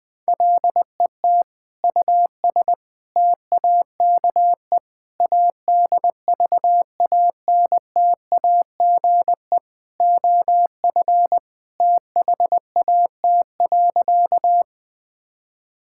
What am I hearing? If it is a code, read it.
Morse code, heard as a keyed tone, about 20 wpm, 700 Hz. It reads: LET US TAKE ADVANTAGE OF THAT.
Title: Sentences from Top 1000 Words - Review Part 6 - 20wpm